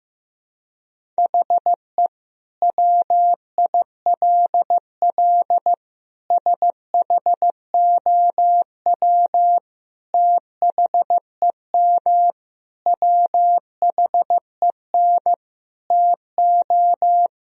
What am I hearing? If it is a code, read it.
Morse code, heard as a keyed tone, about 15 wpm, 700 Hz. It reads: HE WILL SHOW THEM WHEN TO
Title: Sentences from Top 1000 Words - Review Part 1 - 15wpm